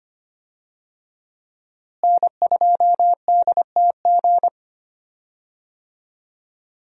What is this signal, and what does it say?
Morse code, heard as a keyed tone, about 25 wpm, 700 Hz. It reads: N2DTG
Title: USA Call Signs - 5 Characters - Repeated 3x - 25wpm